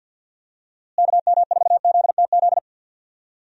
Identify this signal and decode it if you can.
Morse code, heard as a keyed tone, about 51 wpm, 700 Hz. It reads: KM4ZTZ
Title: USA Call Signs - 6 Characters - Repeated 3x - 50wpm